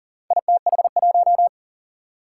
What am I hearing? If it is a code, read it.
Morse code, heard as a keyed tone, about 42 wpm, 700 Hz. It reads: ITH1
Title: Four Challenging Letter-Number Combinations 40wpm